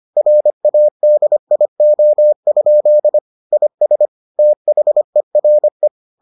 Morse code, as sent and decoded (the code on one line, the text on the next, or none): .-. .- -.. .. --- ..--.. / .. ... / - .... .-. .
RADIO? IS THERE